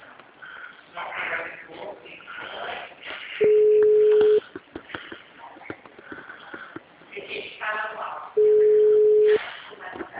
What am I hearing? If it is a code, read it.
Ringback tone (Europe, 425 Hz), with speech.